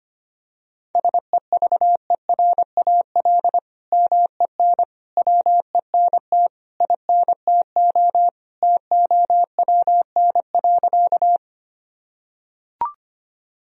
Morse code, heard as a keyed tone, about 25 wpm, 700 Hz. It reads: SEVERAL MEN WENT INTO TOWN.
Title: Sentences from Top 1000 Words - Review Part 2 - 25wpm